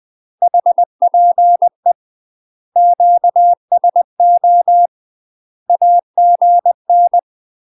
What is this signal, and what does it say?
Morse code, heard as a keyed tone, about 20 wpm, 700 Hz. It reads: HPE QSO AGN
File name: UK MASTER QSOs TRAINING / 20WPM / UK MASTER QSOs 0042